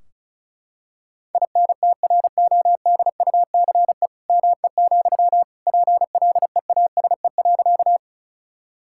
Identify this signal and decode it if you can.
Morse code, heard as a keyed tone, about 36 wpm, 690 Hz. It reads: INTRODUCE ME, PLEASE.
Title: Sentences from Top 1600 Words 35wpm